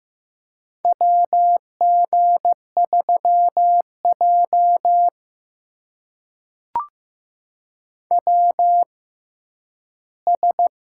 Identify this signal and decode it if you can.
Morse code, heard as a keyed tone, about 15 wpm, spaced as for 10 wpm, 700 Hz. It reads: WG3J WS
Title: USA Call Signs - 4 Characters - 6x Character Spacing - 15wpm